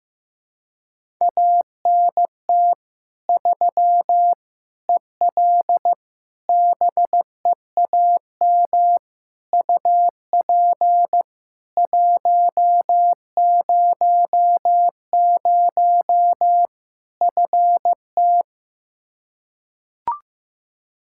Morse code, heard as a keyed tone, about 15 wpm, 700 Hz. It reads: ANT 3 EL BEAM UP 100 FT
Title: K1USN SST QSO Practice 15wpm